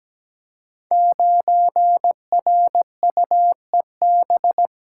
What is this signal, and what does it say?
Morse code, heard as a keyed tone, about 17 wpm, 700 Hz. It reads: 9RUEB
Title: Code Groups - 5 Letter:Numbers - 17wpm